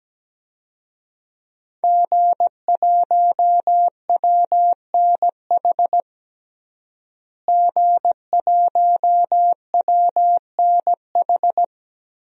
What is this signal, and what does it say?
Morse code, heard as a keyed tone, about 17 wpm, 700 Hz. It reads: G1WNH G1WNH
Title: DXCC-100 Call Signs - 5 Characters - Repeated 3x - 17wpm